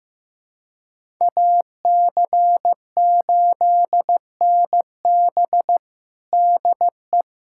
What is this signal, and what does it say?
Morse code, heard as a keyed tone, about 15 wpm, 700 Hz. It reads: AC8NB DE
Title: USA Callsign DE Callsign 15wpm